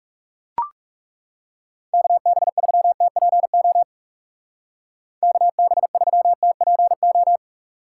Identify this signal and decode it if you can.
Morse code, heard as a keyed tone, about 37 wpm, 700 Hz. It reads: KB3TPO KB3TPO
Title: Speed Racing - USA 2x3 Call Signs 30wpm